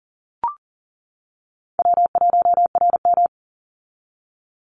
Morse code, heard as a keyed tone, about 40 wpm, 700 Hz. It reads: W1RM
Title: Call Signs - 40wpm